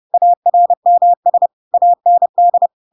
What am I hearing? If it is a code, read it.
Morse code, heard as a keyed tone, about 30 wpm, 700 Hz. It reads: ARMS AND
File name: Pinocchio / Pinocchio ch3 30 30 ik1hge